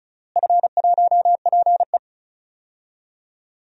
Morse code, heard as a keyed tone, about 34 wpm, 700 Hz. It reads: F1PE
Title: DXCC-100 Call Signs - 4 Characters - Repeated 3x - 35wpm